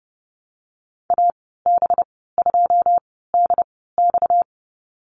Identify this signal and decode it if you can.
Morse code, heard as a keyed tone, about 30 wpm, spaced as for 17 wpm, 700 Hz. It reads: AB2DX
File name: Lesson 37g - Letter x - Call Signs 30wpm